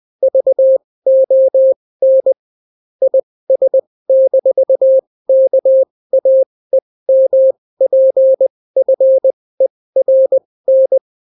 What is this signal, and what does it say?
Morse code, heard as a keyed tone, about 20 wpm, spaced as for 16 wpm, 530 Hz. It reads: VON IS-KAEMPFERN